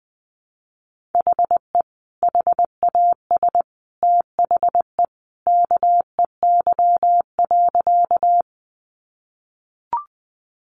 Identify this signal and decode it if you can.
Morse code, heard as a keyed tone, about 20 wpm, 700 Hz. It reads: HE HAS THE KEY.
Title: Sentences from Top 600 Words 20wpm